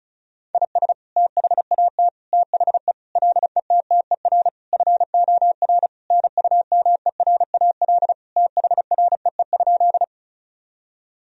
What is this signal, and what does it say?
Morse code, heard as a keyed tone, about 36 wpm, 700 Hz. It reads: IS THAT THE LETTER FOR NUMERAL THREE?
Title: Sentences from Top 1000 Words - Review Part 2 - 35wpm